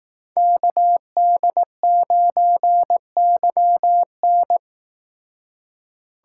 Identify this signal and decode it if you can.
Morse code, heard as a keyed tone, about 18 wpm, 700 Hz. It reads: KD9YN
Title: Speed Racing - USA Call Signs 15wpm